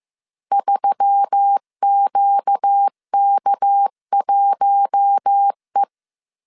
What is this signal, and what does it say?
Morse code, heard as a keyed tone, about 15 wpm, 800 Hz. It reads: 3QK1E
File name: CW / DET / 1 6 1500C S C 14wpm 2951